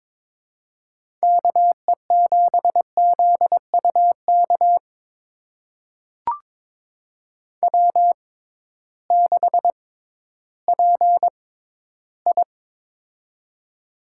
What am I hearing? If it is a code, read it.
Morse code, heard as a keyed tone, about 21 wpm, spaced as for 13 wpm, 700 Hz. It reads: KE7ZUK W6PI
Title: USA Call Signs - 6x Character Spacing - 22wpm